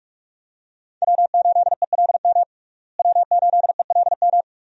Morse code, heard as a keyed tone, about 46 wpm, 700 Hz. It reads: W8ELM W8ELM
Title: USA Call Signs - 5 Characters - Repeated 2x - 45wpm